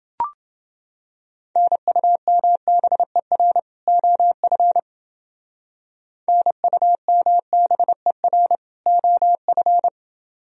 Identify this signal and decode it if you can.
Morse code, heard as a keyed tone, about 30 wpm, 700 Hz. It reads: NUMBER OF NUMBER OF